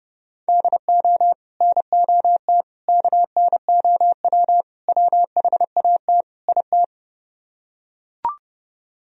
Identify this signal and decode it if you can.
Morse code, heard as a keyed tone, about 28 wpm, 700 Hz. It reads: DO NOT KNOW WHAT IT